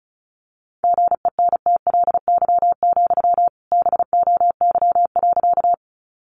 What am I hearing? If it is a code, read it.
Morse code, heard as a keyed tone, about 34 wpm, 700 Hz. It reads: GENTLY, BOY.
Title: Sentences from Top 1500 Words 35wpm